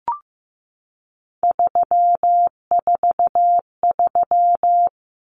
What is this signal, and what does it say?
Morse code, heard as a keyed tone, about 15 wpm, 700 Hz. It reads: 343
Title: Common Cut Numbers - 3-Digit 15wpm